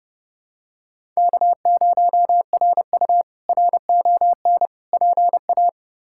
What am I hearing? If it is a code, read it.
Morse code, heard as a keyed tone, about 32 wpm, 700 Hz. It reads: K0RU RON PA
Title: K1USN SST QSO Practice - 30wpm